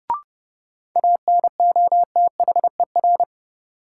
Morse code, heard as a keyed tone, about 32 wpm, 700 Hz. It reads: ANOTHER
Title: Sentences from Top 600 Words - New Word Drill - 30wpm